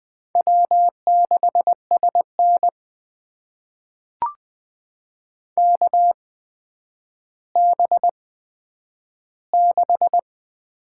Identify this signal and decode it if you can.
Morse code, heard as a keyed tone, about 20 wpm, spaced as for 10 wpm, 700 Hz. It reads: W6SN KB6
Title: USA Call Signs - 8x Character Spacing - 20wpm